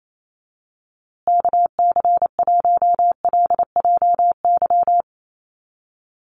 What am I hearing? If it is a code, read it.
Morse code, heard as a keyed tone, about 28 wpm, 700 Hz. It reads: KC1LJY
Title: USA Call Signs Repeated 3x 28wpm